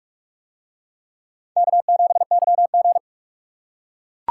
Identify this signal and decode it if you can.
Morse code, heard as a keyed tone, about 43 wpm, 700 Hz. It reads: K7YG